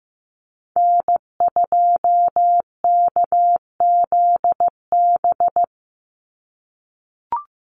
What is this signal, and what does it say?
Morse code, heard as a keyed tone, about 15 wpm, 700 Hz. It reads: N2KZB
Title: USA Call Signs 15wpm